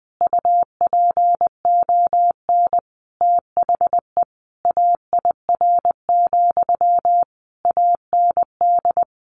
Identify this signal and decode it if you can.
Morse code, heard as a keyed tone, about 20 wpm, 700 Hz. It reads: UPON THE AIR, AND
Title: A Sound Of Thunder 20wpm